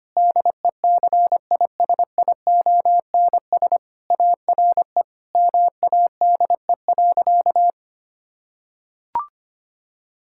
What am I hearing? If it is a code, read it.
Morse code, heard as a keyed tone, about 25 wpm, 700 Hz. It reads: DECISIONS ARE MADE.
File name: practice / 25wpm / Sentences with 4-Word Phrases - 25wpm